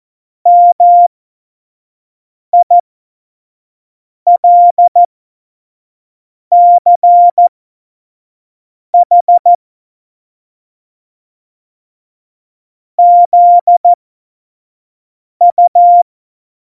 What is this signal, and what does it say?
Morse code, heard as a keyed tone, about 14 wpm, spaced as for 5 wpm, 700 Hz. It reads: MILCH ZU
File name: Witz 31. Oktober 17, Tempo 25